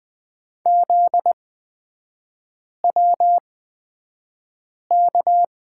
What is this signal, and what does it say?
Morse code, heard as a keyed tone, about 20 wpm, spaced as for 5 wpm, 700 Hz. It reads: ZWK
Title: 243 Lesson 39e - Letter z - Three Characters 20wpm